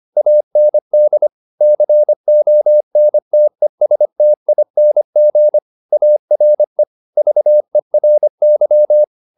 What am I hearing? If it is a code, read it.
Morse code, heard as a keyed tone, about 25 wpm, 600 Hz. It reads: AND CONTESTING ARE VERY